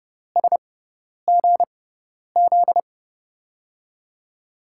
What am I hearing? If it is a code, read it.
Morse code, heard as a keyed tone, about 30 wpm, spaced as for 10 wpm, 700 Hz. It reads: SGZ
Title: USA Call Signs - 5 Characters - 6x Character Spacing - 30wpm